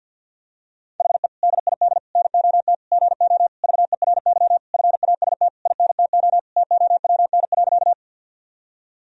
Morse code, heard as a keyed tone, about 49 wpm, 700 Hz. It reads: HE DID NOT GO VERY FAST INTO TOWN.
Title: Sentences from Top 1000 Words - Review Part 2 - 50wpm